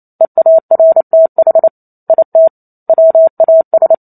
Morse code, heard as a keyed tone, about 28 wpm, 650 Hz. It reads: EARTH IT WAS